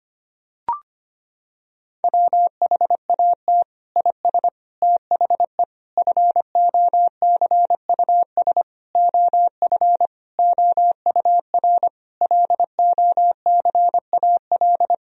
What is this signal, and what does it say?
Morse code, heard as a keyed tone, about 25 wpm, 700 Hz. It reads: WHAT IS THE FOCUS OF OUR LOCAL
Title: Sentences from Top 1000 Words - Review Part 4 - 25wpm